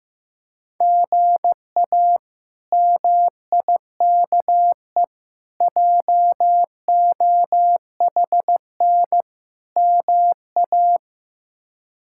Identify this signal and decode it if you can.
Morse code, heard as a keyed tone, about 15 wpm, 700 Hz. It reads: GA MIKE JOHN MA
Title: K1USN SST QSO Practice - 15wpm